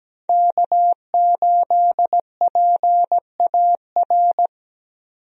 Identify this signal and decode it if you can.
Morse code, heard as a keyed tone, about 17 wpm, 700 Hz. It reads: K8PAR
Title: USA Call Signs - 5 Characters - 17wpm